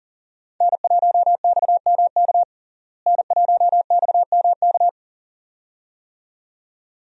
Morse code, heard as a keyed tone, about 40 wpm, 700 Hz. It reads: N1XMK N1XMK
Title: USA Call Signs - 5 Characters - Repeated 2x - 40wpm